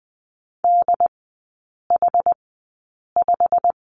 Morse code, heard as a keyed tone, about 20 wpm, spaced as for 8 wpm, 700 Hz. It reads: DH5
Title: 093 Lesson 16e - Letter c - Three Characters 20wpm